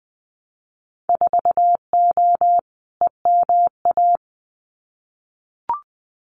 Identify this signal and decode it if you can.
Morse code, heard as a keyed tone, about 20 wpm, 700 Hz. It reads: 4O EMA
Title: Winter Field Day Exchanges 20wpm